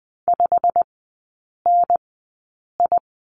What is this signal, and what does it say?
Morse code, heard as a keyed tone, about 20 wpm, spaced as for 8 wpm, 700 Hz. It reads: 5NI
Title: 100 Lesson 17e - Letter u - Three Characters 20wpm